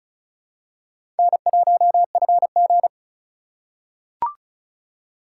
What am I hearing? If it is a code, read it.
Morse code, heard as a keyed tone, about 34 wpm, 700 Hz. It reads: N1FG
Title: USA Call Signs - 4 Characters - 2x Character Spacing - 35wpm